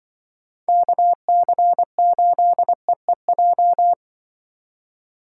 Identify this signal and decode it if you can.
Morse code, heard as a keyed tone, about 24 wpm, 700 Hz. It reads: KC8EEJ
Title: Speed Racing - USA 2x3 Call Signs 20wpm